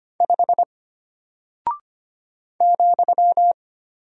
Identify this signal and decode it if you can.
Morse code, heard as a keyed tone, about 25 wpm, 700 Hz. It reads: 5 ,
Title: Single Character (Letter:Number:Punctuation:Prosign) Alt 25wpm